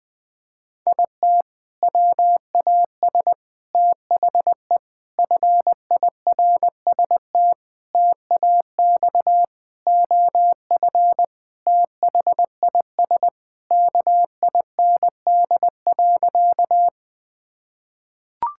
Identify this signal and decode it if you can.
Morse code, heard as a keyed tone, about 20 wpm, 700 Hz. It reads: IT WAS THE FIRST TAX OF THIS KIND.
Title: Sentences from Top 1000 Words - Review Part 5 - 20wpm